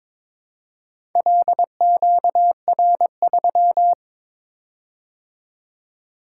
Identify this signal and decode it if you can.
Morse code, heard as a keyed tone, about 22 wpm, 700 Hz. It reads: LQR3
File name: practice / 22wpm / Four Letter-Number Combinations - 22wpm